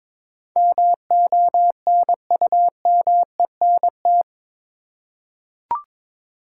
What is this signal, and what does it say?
Morse code, heard as a keyed tone, about 21 wpm, 700 Hz. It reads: MONUMENT
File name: practice / 22wpm / Eight Letter Words - 22wpm